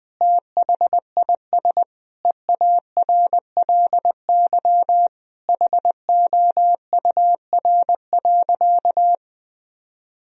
Morse code, heard as a keyed tone, about 20 wpm, 700 Hz. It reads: THIS EARLY HOUR.
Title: Sentences from Top 1000 Words - Review Part 2 - 20wpm